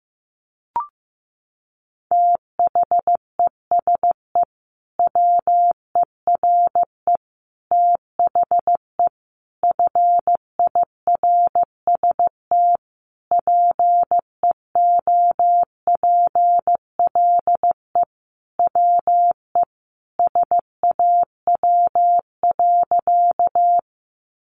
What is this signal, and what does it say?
Morse code, heard as a keyed tone, about 15 wpm, 700 Hz. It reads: THESE WERE THE FIRST PEOPLE WE SAW.
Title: Sentences from Top 300 Words 15wpm